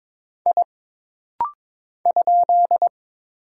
Morse code, heard as a keyed tone, about 22 wpm, 700 Hz. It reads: I ?
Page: This rapid-fire timing is geared to push your ICR (Instant Character Recognition) skills.